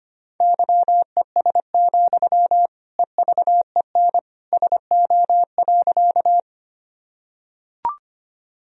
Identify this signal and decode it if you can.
Morse code, heard as a keyed tone, about 25 wpm, 700 Hz. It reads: YES, EVEN SO.
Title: Sentences from Top 1000 Words - Review Part 1 - 25wpm